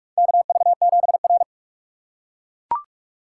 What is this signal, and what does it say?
Morse code, heard as a keyed tone, about 44 wpm, 700 Hz. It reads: KV7R
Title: USA Call Signs - 4 Characters - 4x Character Spacing - 45wpm